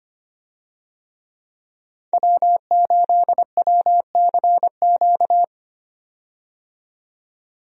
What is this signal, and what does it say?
Morse code, heard as a keyed tone, about 25 wpm, 700 Hz. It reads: W8WCQ
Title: USA Call Signs - 5 Characters - 25wpm